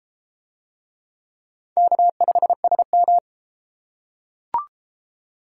Morse code, heard as a keyed tone, about 32 wpm, 700 Hz. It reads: K5SM